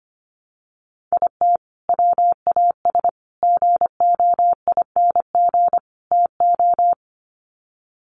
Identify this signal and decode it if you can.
Morse code, heard as a keyed tone, about 25 wpm, 700 Hz. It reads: IT WAS GOING TO
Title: Sets of 4 Words 25wpm